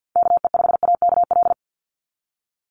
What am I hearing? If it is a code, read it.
Morse code, heard as a keyed tone, about 50 wpm, 700 Hz. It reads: KE5AKL
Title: NA SOTA Chaser Callsigns 50wpm